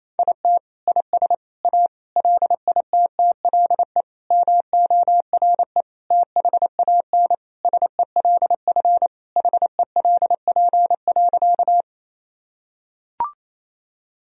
Morse code, heard as a keyed tone, about 28 wpm, 700 Hz. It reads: IT IS A LITTLE MORE THAN SELF HELP.
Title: Sentences from Top 1000 Words - Review Part 1 - 28wpm